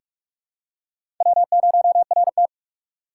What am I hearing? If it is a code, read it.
Morse code, heard as a keyed tone, about 44 wpm, 700 Hz. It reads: W0RT
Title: USA Call Signs - 4 Characters - 6x Character Spacing - 45wpm